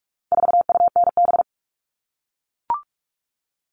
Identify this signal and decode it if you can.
Morse code, heard as a keyed tone, about 44 wpm, 700 Hz. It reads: 4UNB
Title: All Cut Numbers - 4-Digit 45wpm